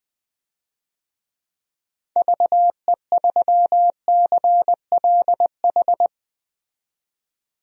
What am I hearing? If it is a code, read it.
Morse code, heard as a keyed tone, about 20 wpm, 700 Hz. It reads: VE3CLH
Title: DXCC-100 Call Signs - 6 Characters - 20wpm